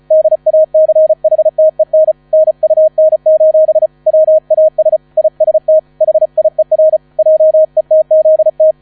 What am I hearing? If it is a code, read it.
Morse code, heard as a keyed tone, about 35 wpm, 630 Hz, with radio noise. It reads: DACHTEN NUN: WAS IST HIER JETZT
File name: 35wpm